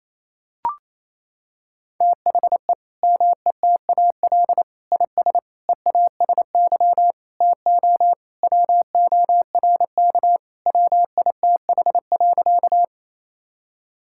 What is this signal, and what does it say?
Morse code, heard as a keyed tone, about 28 wpm, 700 Hz. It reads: THE METAL IS EASY TO WORK WITH.
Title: Sentences from Top 1000 Words - Review Part 6 - 28wpm